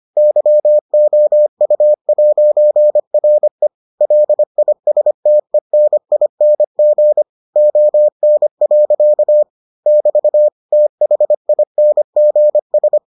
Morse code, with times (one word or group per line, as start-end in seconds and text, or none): 0.2-3.7 YOU'RE
4.0-7.2 LISTENING
7.6-9.4 ON.
9.9-10.5 <BT>
10.7-13.0 THINGS